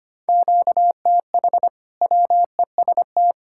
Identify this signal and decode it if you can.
Morse code, heard as a keyed tone, about 25 wpm, 700 Hz. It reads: QTH WEST
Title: Practice QSO - Exchange 1 - Beginning - 25wpm